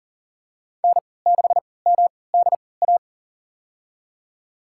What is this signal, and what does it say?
Morse code, heard as a keyed tone, about 40 wpm, spaced as for 23 wpm, 700 Hz. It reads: N6MDA
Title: USA Call Signs - 5 Characters - 3x Character Spacing - 40wpm